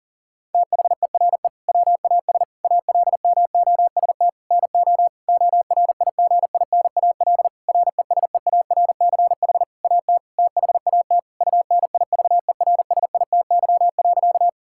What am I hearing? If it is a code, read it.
Morse code, heard as a keyed tone, about 40 wpm, 700 Hz. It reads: THERE WAS ALMOST NO ORIGINAL RESEARCH AT THAT UNIVERSITY.
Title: Sentences from Top 1000 Words - Review Part 6 - 40wpm